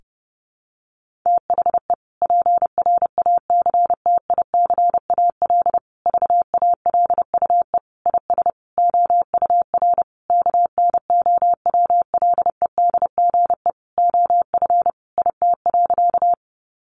Morse code, heard as a keyed tone, about 30 wpm, 700 Hz. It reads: THE PRACTICAL VALUE IS OUR KNOWLEDGE OF IT.
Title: Sentences from Top 1400 Words 30wpm